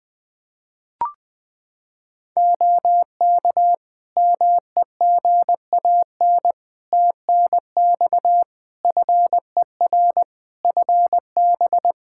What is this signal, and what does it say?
Morse code, heard as a keyed tone, about 20 wpm, 700 Hz. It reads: OK MEGAN TNX FER FB
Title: Practice QSO - Exchange 4 - Ending - 20wpm